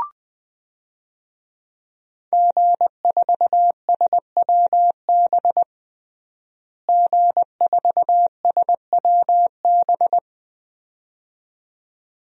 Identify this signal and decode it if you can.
Morse code, heard as a keyed tone, about 21 wpm, 700 Hz. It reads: G4SWB G4SWB
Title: DXCC-100 Call Signs - 5 Characters - Repeated 2x - 20wpm